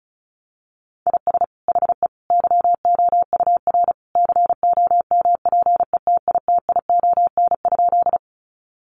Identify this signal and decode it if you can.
Morse code, heard as a keyed tone, about 34 wpm, 700 Hz. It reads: IS HE YOUR COMPETITION?
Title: Sentences from Top 1000 Words 35wpm